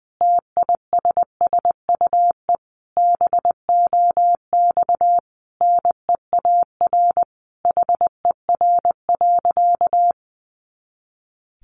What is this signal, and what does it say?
Morse code, heard as a keyed tone, about 20 wpm, 700 Hz. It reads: TISSUE BOX NEAR HER.
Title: Easy Phrases 20wpm